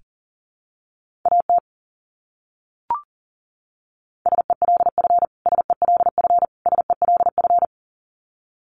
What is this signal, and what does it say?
Morse code, heard as a keyed tone, about 40 wpm, 700 Hz. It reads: AT SELF SELF SELF
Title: Top 200 Words Repeated 40wpm